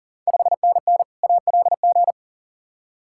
Morse code, heard as a keyed tone, about 40 wpm, 690 Hz. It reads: 5NN ALG
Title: Ontario QSO Party Exchange 40wpm